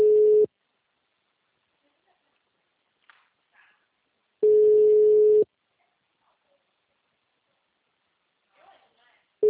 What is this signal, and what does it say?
Ringback tone (Europe, 425 Hz), with speech.
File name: out- -2002- - - .432